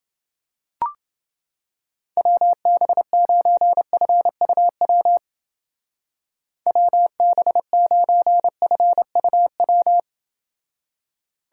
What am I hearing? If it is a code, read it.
Morse code, heard as a keyed tone, about 29 wpm, 700 Hz. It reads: WB9FUW WB9FUW